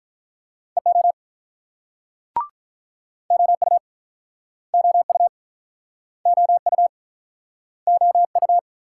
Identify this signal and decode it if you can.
Morse code, heard as a keyed tone, about 49 wpm, 700 Hz. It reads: EO OU OU OU OU